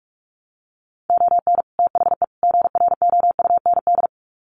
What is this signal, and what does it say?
Morse code, heard as a keyed tone, about 44 wpm, 700 Hz. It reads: ON THE GROUND
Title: Sets of 3 Words 45wpm